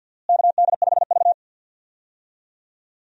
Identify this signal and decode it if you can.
Morse code, heard as a keyed tone, about 49 wpm, 700 Hz. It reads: KD5V
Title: USA Call Signs - 4 Characters - Repeated 3x - 50wpm